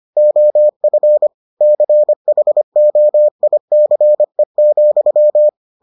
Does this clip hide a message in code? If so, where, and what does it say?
Yes: Morse code from 0.2 to 5.5 s, heard as a keyed tone, about 25 wpm, 600 Hz. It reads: OF CHOICE,